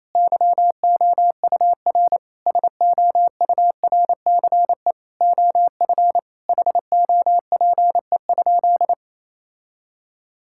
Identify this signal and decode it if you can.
Morse code, heard as a keyed tone, about 28 wpm, 700 Hz. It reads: YOUR SOURCE OF HOPE?
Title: Sentences from Top 1000 Words - Review Part 5 - 28wpm